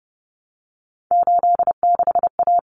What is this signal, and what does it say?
Morse code, heard as a keyed tone, about 30 wpm, 700 Hz. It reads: 86A
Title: Common Cut Numbers - 3-Digit 30wpm